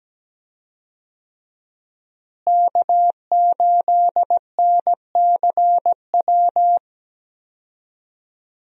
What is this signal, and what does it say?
Morse code, heard as a keyed tone, about 17 wpm, 700 Hz. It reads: K8NCW
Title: USA Call Signs - 5 Characters - Repeated 2x - 17wpm